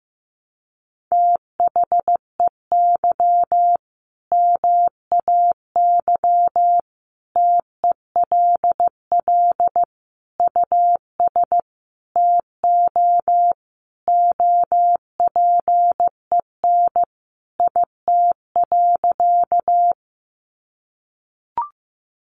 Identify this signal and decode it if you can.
Morse code, heard as a keyed tone, about 15 wpm, 700 Hz. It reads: THEY MAY TELL US TO OPEN IT.
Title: Sentences from Top 300 Words 15wpm